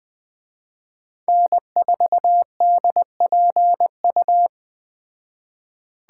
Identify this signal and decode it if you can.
Morse code, heard as a keyed tone, about 20 wpm, 700 Hz. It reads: N4DPU
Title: Speed Racing - USA 1x3 Call Signs 17wpm